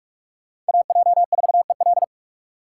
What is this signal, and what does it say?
Morse code, heard as a keyed tone, about 42 wpm, 700 Hz. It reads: AJ4EL
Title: USA Call Signs - 3x Character Spacing - 45wpm